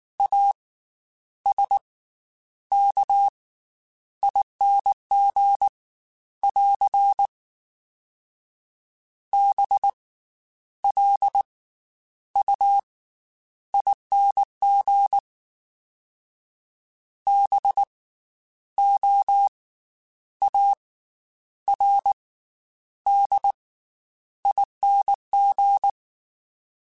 Morse code, heard as a keyed tone, about 19 wpm, 800 Hz. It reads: A S K ING <AR> B L U ING B O A R D ING
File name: Special INGs / Special INGs-mp3s / Slow ing @18